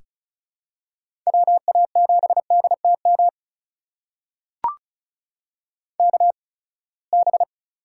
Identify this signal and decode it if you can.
Morse code, heard as a keyed tone, about 34 wpm, spaced as for 24 wpm, 700 Hz. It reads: WA7DTM KB
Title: USA Call Signs - 6 Characters - 8x Character Spacing - 35wpm